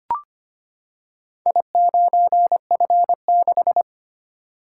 Morse code, heard as a keyed tone, about 25 wpm, 700 Hz. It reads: I9F6
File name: practice / 25wpm / Four Letter-Number Combinations - 25wpm